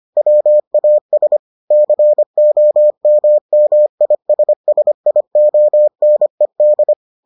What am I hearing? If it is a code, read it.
Morse code, heard as a keyed tone, about 25 wpm, 600 Hz. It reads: WAS COMMISSIONED